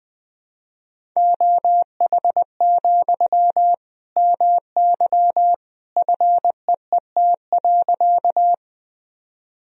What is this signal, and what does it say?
Morse code, heard as a keyed tone, about 21 wpm, 700 Hz. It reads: OH, MY FEET.
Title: Sentences from Top 1000 Words - Review Part 1 - 20wpm